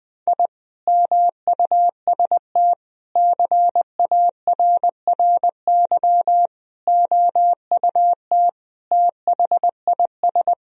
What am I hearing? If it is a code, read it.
Morse code, heard as a keyed tone, about 20 wpm, 700 Hz. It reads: I MUST CARRY OUT THIS
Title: Sentences from Top 1000 Words - Review Part 2 - 20wpm